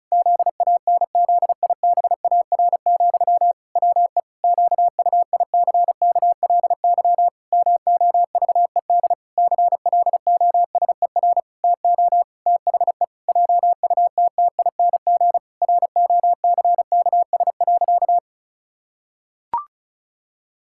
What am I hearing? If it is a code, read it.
Morse code, heard as a keyed tone, about 36 wpm, 700 Hz. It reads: ZANZIBAR, WE QUICKLY MOVED CLOSER TO THE JUTTING ROCKS.
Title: Pangram Challenge v2 - 35wpm